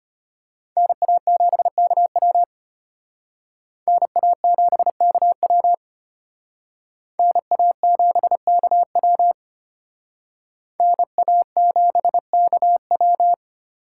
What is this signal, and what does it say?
Morse code, heard as a keyed tone, about 32 wpm, 700 Hz. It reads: NA7KW NA7KW NA7KW NA7KW